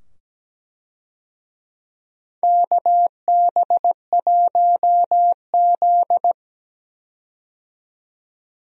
Morse code, heard as a keyed tone, about 17 wpm, 700 Hz. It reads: KB1Z
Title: USA Call Signs - 4 Characters - 17wpm